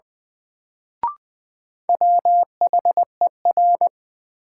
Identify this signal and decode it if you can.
Morse code, heard as a keyed tone, about 20 wpm, 700 Hz. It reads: WHER
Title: Sentences from Top 1500 Words - New Word Drill Alt - 20wpm